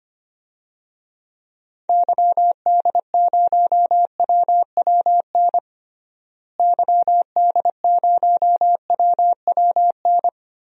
Morse code, heard as a keyed tone, about 25 wpm, 700 Hz. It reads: YD0WWN YD0WWN
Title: DXCC-100 Call Signs - 6 Characters - Repeated 3x - 25wpm